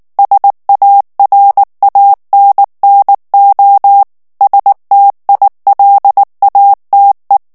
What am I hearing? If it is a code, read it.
Morse code, heard as a keyed tone, about 19 wpm, 800 Hz. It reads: SARANNO STILATE
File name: morse mp3 / ITALIEN texte clair 2 vitesse 18 mots minute